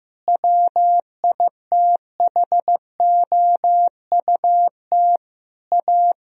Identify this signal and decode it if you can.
Morse code, heard as a keyed tone, about 15 wpm, 700 Hz. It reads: WITHOUT A
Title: Sentences from Top 1000 Words - Review Part 3 - 15wpm